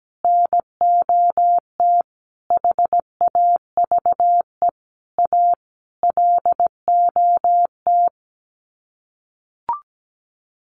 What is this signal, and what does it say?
Morse code, heard as a keyed tone, about 17 wpm, 700 Hz. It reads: NOT HAVE A LOT
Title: Sets of 5 Words Encore 17wpm